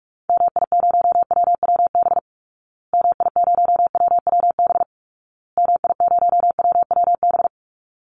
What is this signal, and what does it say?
Morse code, heard as a keyed tone, about 44 wpm, 700 Hz. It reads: MI0WWB MI0WWB MI0WWB
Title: Call Signs Repeated 2x - 45wpm